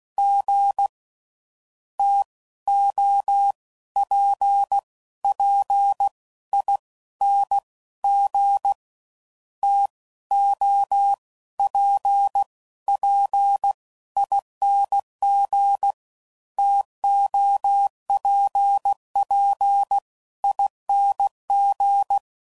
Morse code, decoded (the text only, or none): G TOPPING TOPPING TOPPING